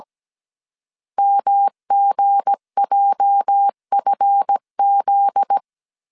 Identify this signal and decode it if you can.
Morse code, heard as a keyed tone, about 17 wpm, 800 Hz. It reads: MGJFZ
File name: CW / DET / 1 9 1500C S C 16wpm 2909